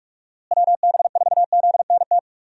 Morse code, heard as a keyed tone, about 42 wpm, 700 Hz. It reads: WB4ZNT
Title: USA Call Signs - 6 Characters - Repeated 2x - 45wpm